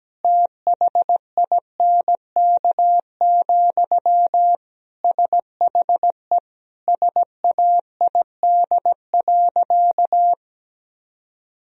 Morse code, heard as a keyed tone, about 17 wpm, 700 Hz. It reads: THINK, SHE SAID.
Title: Sentences from Top 1000 Words - Review Part 4 - 17wpm